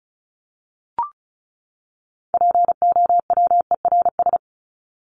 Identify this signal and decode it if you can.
Morse code, heard as a keyed tone, about 34 wpm, 700 Hz. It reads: POWERS
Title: ES:S Suffix - Top 100 Words with Base Words 35wpm